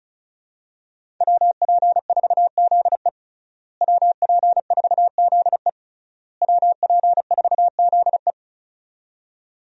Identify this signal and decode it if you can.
Morse code, heard as a keyed tone, about 36 wpm, 700 Hz. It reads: WP4ZE WP4ZE WP4ZE
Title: USA Call Signs - 5 Characters - Repeated 3x - 35wpm